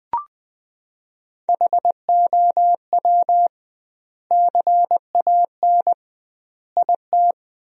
Morse code, heard as a keyed tone, about 20 wpm, 700 Hz. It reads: HOW CAN IT
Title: 2x Word-Spacing - Sentences from Top 100 Words - 20wpm